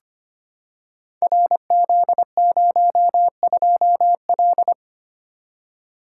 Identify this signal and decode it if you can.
Morse code, heard as a keyed tone, about 25 wpm, 700 Hz. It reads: RZ02L
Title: Code Groups - 5 Letter:Numbers - 25wpm